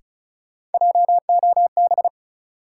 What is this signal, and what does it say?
Morse code, heard as a keyed tone, about 36 wpm, 700 Hz. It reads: JOB
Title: Sentences from Top 600 Words - New Word Drill - 35wpm